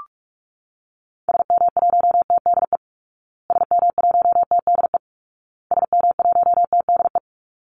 Morse code, heard as a keyed tone, about 44 wpm, 700 Hz. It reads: SM1TDE SM1TDE SM1TDE
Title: Call Signs Repeated 2x - 45wpm